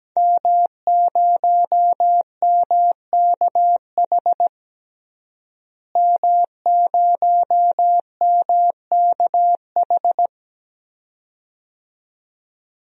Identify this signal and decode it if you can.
Morse code, heard as a keyed tone, about 17 wpm, 700 Hz. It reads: M0MKH M0MKH
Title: DXCC-100 Call Signs - 5 Characters - Repeated 2x - 17wpm